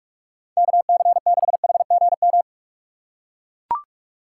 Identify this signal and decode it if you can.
Morse code, heard as a keyed tone, about 44 wpm, 700 Hz. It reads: KC6HGM